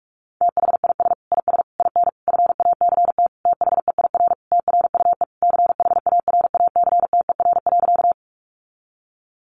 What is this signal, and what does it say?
Morse code, heard as a keyed tone, about 44 wpm, 700 Hz. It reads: THIS IS IN FACT THEIR TRUE CHARACTER.
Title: Sentences from Top 1200 Words 45wpm